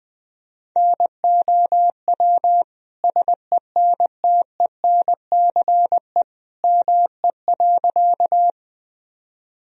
Morse code, heard as a keyed tone, about 20 wpm, 700 Hz. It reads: NOW SENTENCE ME.
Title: Sentences from Top 1000 Words - Review Part 1 - 20wpm